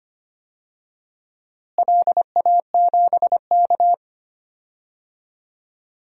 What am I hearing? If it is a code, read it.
Morse code, heard as a keyed tone, about 25 wpm, 700 Hz. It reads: LA7K
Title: DXCC-100 Call Signs - 4 Characters - Repeated 2x - 25wpm